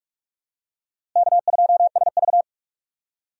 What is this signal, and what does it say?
Morse code, heard as a keyed tone, about 46 wpm, 700 Hz. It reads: K2SV
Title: USA Call Signs - 4 Characters - Repeated 3x - 45wpm